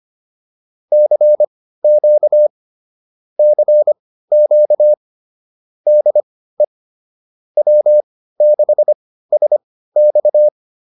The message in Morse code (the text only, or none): CQ CQ DE W6SX